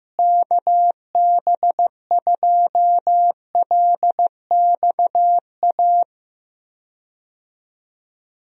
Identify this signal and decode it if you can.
Morse code, heard as a keyed tone, about 15 wpm, 700 Hz. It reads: KB2LXA
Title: USA Call Signs - 6 Characters - Repeated 3x - 15wpm